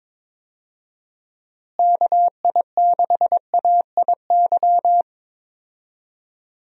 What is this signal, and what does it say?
Morse code, heard as a keyed tone, about 22 wpm, 700 Hz. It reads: KI6AIY
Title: USA Call Signs - 6 Characters - Repeated 3x - 22wpm